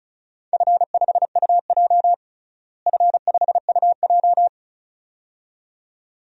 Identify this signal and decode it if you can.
Morse code, heard as a keyed tone, about 36 wpm, 690 Hz. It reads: F5UJ F5UJ
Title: DXCC-100 Call Signs - 4 Characters - Repeated 2x - 35wpm